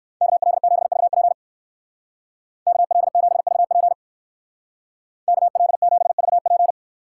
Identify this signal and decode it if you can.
Morse code, heard as a keyed tone, about 57 wpm, 690 Hz. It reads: KL7VP KL7VP KL7VP